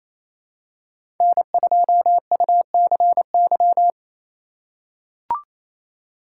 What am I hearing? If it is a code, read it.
Morse code, heard as a keyed tone, about 28 wpm, 690 Hz. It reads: N2UCY
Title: USA Call Signs - 2x Character Spacing - 28wpm